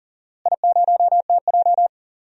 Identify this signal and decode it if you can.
Morse code, heard as a keyed tone, about 37 wpm, 700 Hz. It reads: I0TJ
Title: DXCC-100 Call Signs - 4 Characters - Repeated 3x - 40wpm